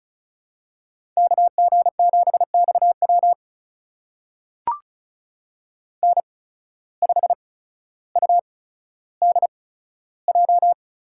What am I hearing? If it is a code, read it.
Morse code, heard as a keyed tone, about 36 wpm, spaced as for 15 wpm, 700 Hz. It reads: KG7XW N5UDJ
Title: USA Call Signs - 5 Characters - 8x Character Spacing - 35wpm